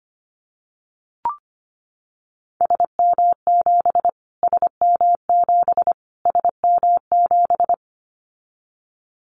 Morse code, heard as a keyed tone, about 25 wpm, 700 Hz. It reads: SM7 SM7 SM7
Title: Callsign Prefixes for Top 25 DXCC - Repeated 3x - 25wpm